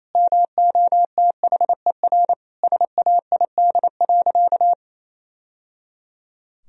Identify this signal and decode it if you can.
Morse code, heard as a keyed tone, about 27 wpm, 700 Hz. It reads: MOTHER SAID.
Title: Sentences from Top 1000 Words - Review Part 3 - 28wpm